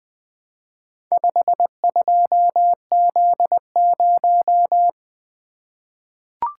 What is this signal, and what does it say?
Morse code, heard as a keyed tone, about 20 wpm, 700 Hz. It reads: 52Z0
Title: Four Letter-Number Combinations - 20wpm